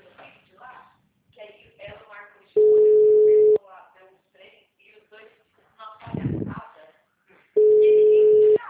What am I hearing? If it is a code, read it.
Ringback tone (Europe, 425 Hz), with speech.